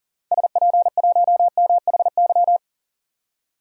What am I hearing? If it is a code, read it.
Morse code, heard as a keyed tone, about 37 wpm, 700 Hz. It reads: SP1MHY
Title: DXCC-100 Call Signs - 6 Characters - Repeated 3x - 40wpm